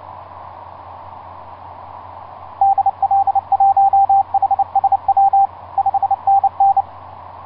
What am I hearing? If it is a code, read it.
Morse code, heard as a keyed tone, about 29 wpm, 790 Hz, with radio noise. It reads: DL1HSW 5NN
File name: s79gm s59z etc 10m cw 30mar2010 1126